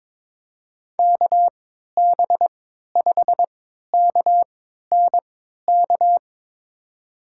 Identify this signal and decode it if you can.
Morse code, heard as a keyed tone, about 21 wpm, spaced as for 12 wpm, 700 Hz. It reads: KB5KNK
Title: USA Call Signs - 6 Characters - 3x Character Spacing - 22wpm